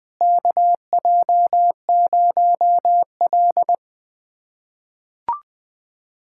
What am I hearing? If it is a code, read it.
Morse code, heard as a keyed tone, about 20 wpm, 700 Hz. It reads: KJ0L
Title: USA Callsign DE Callsign - 4 Characters - 20wpm